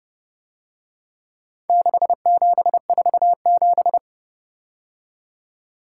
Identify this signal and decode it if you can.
Morse code, heard as a keyed tone, about 30 wpm, 700 Hz. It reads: 6747